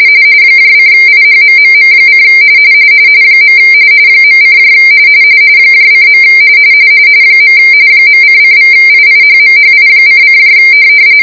RTTY, about 45.45 baud. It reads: :IHMMM: . .: :. .::=IIHMHI::